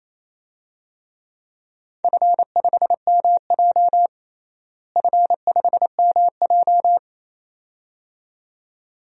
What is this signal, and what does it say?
Morse code, heard as a keyed tone, about 28 wpm, 700 Hz. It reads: F5MJ F5MJ
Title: DXCC-100 Call Signs - 4 Characters - Repeated 2x - 28wpm